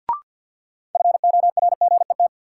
Morse code, heard as a keyed tone, about 50 wpm, 700 Hz. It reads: FORGET
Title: Sentences from Top 1200 Words - New Word Drill Alt - 50wpm